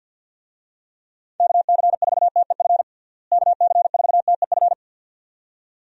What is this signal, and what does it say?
Morse code, heard as a keyed tone, about 51 wpm, 700 Hz. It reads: KC4TEF KC4TEF
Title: USA Call Signs - 6 Characters - Repeated 2x - 50wpm